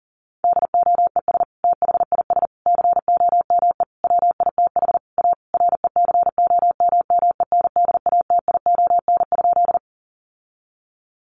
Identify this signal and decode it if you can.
Morse code, heard as a keyed tone, about 37 wpm, 700 Hz. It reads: DOES THIS COME WITH A RECOMMENDATION?
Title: Sentences from Top 1400 Words 40wpm